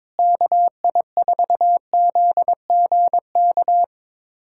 Morse code, heard as a keyed tone, about 22 wpm, 700 Hz. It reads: KI4ZGK
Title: USA Callsign DE Callsign - 6 Characters - 22wpm